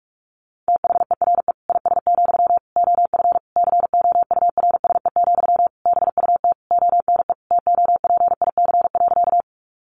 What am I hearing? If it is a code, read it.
Morse code, heard as a keyed tone, about 44 wpm, 700 Hz. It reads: THERE IS, OF COURSE, BUT ONE TOPIC.
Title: Sentences from Top 1200 Words 45wpm